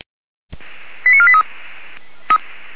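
Five-tone selective call (ZVEI), heard as quick runs of five tones, tones of about 70 ms. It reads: 89483
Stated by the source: ZVEI five-tone selective call received by SDR.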